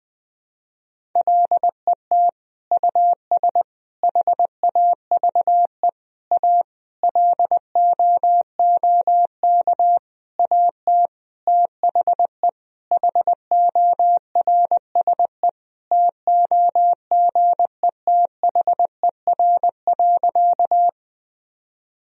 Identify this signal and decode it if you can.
Morse code, heard as a keyed tone, about 19 wpm, 700 Hz. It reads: LET US HAVE A LOOK AT THE HORSE TOGETHER.
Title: Sentences from Top 1000 Words - Review Part 1 - 20wpm